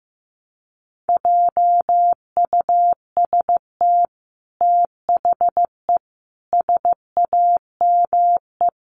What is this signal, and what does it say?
Morse code, heard as a keyed tone, about 15 wpm, 700 Hz. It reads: JUST THE SAME
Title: Sentences from Top 900 Words 15wpm